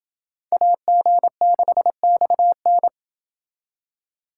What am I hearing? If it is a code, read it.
Morse code, heard as a keyed tone, about 28 wpm, 700 Hz. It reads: AG6XN